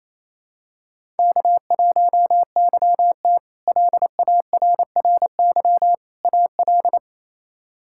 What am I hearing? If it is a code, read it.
Morse code, heard as a keyed tone, about 27 wpm, 700 Hz. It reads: K1YT LARRY AL
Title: K1USN SST QSO Practice - 28wpm